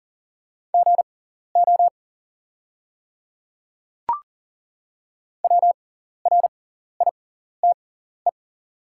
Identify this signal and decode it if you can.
Morse code, heard as a keyed tone, about 40 wpm, spaced as for 14 wpm, 690 Hz. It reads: GO WRITE
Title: 6x Character Spacing - Top 100 Words - 40wpm